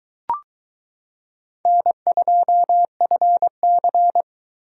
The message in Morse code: N2FC

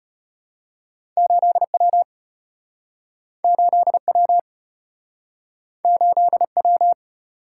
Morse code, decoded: 8W 8W 8W

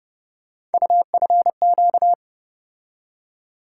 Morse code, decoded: UFQ

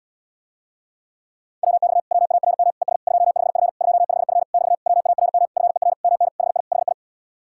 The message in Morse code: WX WARM ES 25F 25F <BT> WARM FER MN HI HI

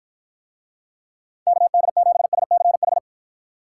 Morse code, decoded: KD7SCH

